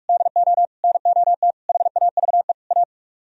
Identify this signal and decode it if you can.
Morse code, heard as a keyed tone, about 41 wpm, 700 Hz. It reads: DO NOT HAVE A